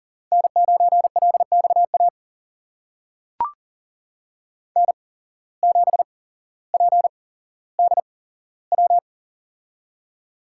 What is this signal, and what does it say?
Morse code, heard as a keyed tone, about 40 wpm, spaced as for 17 wpm, 700 Hz. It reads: N9LXA N7PDW
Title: USA Call Signs - 5 Characters - 8x Character Spacing - 40wpm